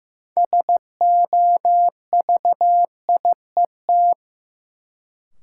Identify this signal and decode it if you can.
Morse code, heard as a keyed tone, about 15 wpm, 700 Hz. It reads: SOVIET
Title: Sets of 5 Words Encore - New Word Drill - 15wpm